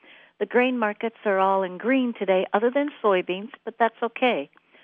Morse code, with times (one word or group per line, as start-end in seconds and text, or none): none